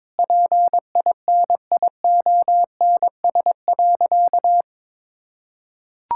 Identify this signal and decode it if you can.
Morse code, heard as a keyed tone, about 22 wpm, 700 Hz. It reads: PINIONS.
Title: Pangram Challenge v2 - 22wpm